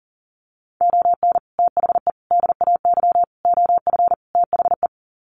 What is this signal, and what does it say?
Morse code, heard as a keyed tone, about 40 wpm, 700 Hz. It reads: ON THE DAY OF THE
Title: Sets of 5 Words Encore 40wpm